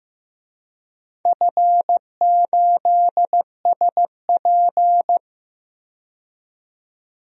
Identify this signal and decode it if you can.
Morse code, heard as a keyed tone, about 15 wpm, 700 Hz. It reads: F8SP
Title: DXCC-100 Call Signs - 4 Characters - 15wpm